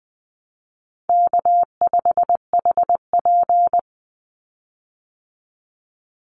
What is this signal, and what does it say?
Morse code, heard as a keyed tone, about 20 wpm, 700 Hz. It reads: K5HP
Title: Call Signs Repeated 3x - 20wpm